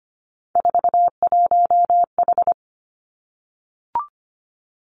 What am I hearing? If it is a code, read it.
Morse code, heard as a keyed tone, about 25 wpm, 700 Hz. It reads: 41H
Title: Three Challenging Letter-Number Combinations 25wpm